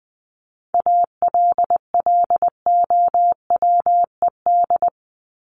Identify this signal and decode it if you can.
Morse code, heard as a keyed tone, about 20 wpm, 700 Hz. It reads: ALLOWED
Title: ED Suffix - Top 100 Words with Base Words 20wpm